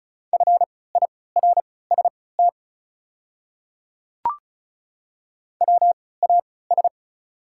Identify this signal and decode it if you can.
Morse code, heard as a keyed tone, about 36 wpm, spaced as for 20 wpm, 700 Hz. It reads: FIRST WAS
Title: 3x Character Spacing - Top 100 Words - 35wpm